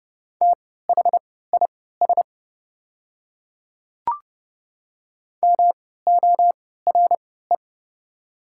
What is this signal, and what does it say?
Morse code, heard as a keyed tone, about 30 wpm, spaced as for 17 wpm, 700 Hz. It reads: THIS MORE
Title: 3x Character Spacing - Top 100 Words - 30wpm